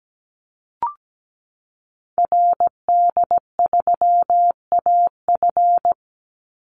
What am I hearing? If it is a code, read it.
Morse code, heard as a keyed tone, about 17 wpm, 700 Hz. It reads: RD3AF
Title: Call Signs Repeated 2x - 17wpm